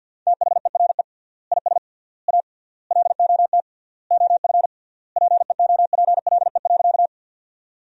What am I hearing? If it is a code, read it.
Morse code, heard as a keyed tone, about 47 wpm, 700 Hz. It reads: THERE IS A LOT OF PEOPLE.
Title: 3x Word-Spacing - Sentences from Top 500 Words - 50wpm